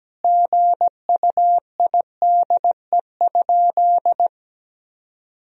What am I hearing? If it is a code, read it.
Morse code, heard as a keyed tone, about 17 wpm, 700 Hz. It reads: GUIDE?
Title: Sentences from Top 1000 Words - Review Part 3 - 17wpm